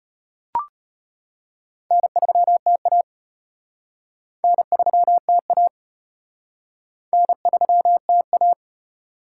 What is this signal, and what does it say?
Morse code, heard as a keyed tone, about 37 wpm, 700 Hz. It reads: N3TA N3TA N3TA